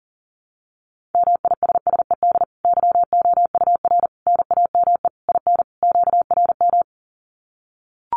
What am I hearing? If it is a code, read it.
Morse code, heard as a keyed tone, about 40 wpm, 700 Hz. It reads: MISSED YOUR NAME IN QRM
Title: Common QSO Phrases 40wpm